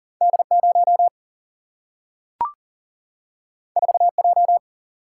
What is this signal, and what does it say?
Morse code, heard as a keyed tone, about 40 wpm, 700 Hz. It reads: D0 4J
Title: Single-Letter Single-Number 40wpm